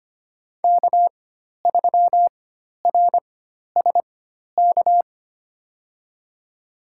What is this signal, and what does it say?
Morse code, heard as a keyed tone, about 25 wpm, spaced as for 12 wpm, 700 Hz. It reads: K3RSK
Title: USA Call Signs - 5 Characters - 4x Character Spacing - 25wpm